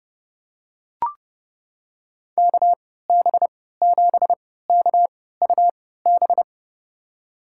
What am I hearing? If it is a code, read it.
Morse code, heard as a keyed tone, about 32 wpm, spaced as for 17 wpm, 700 Hz. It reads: KB7KUB
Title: USA Call Signs - 6 Characters - 3x Character Spacing - 30wpm